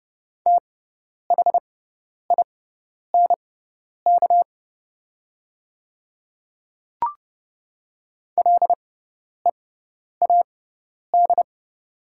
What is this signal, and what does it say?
Morse code, heard as a keyed tone, about 30 wpm, spaced as for 10 wpm, 690 Hz. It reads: THINK LEAD
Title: 6x Character Spacing - Top 500 Words - 30wpm